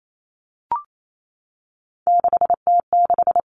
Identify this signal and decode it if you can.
Morse code, heard as a keyed tone, about 28 wpm, 700 Hz. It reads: 6T6
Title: All Cut Numbers - 3-Digit 28wpm